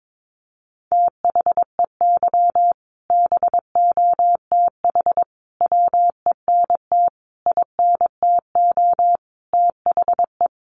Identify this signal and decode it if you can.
Morse code, heard as a keyed tone, about 22 wpm, 700 Hz. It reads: THEY BOTH WENT INTO THE
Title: Sentences from Top 600 Words 22wpm